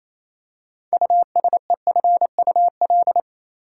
Morse code, heard as a keyed tone, about 28 wpm, 700 Hz. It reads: USEFUL
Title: Six Letter Words - 28wpm